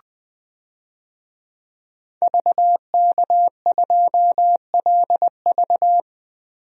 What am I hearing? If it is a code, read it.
Morse code, heard as a keyed tone, about 20 wpm, 700 Hz. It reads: VK2LV
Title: DXCC-100 Call Signs - 5 Characters - Repeated 2x - 20wpm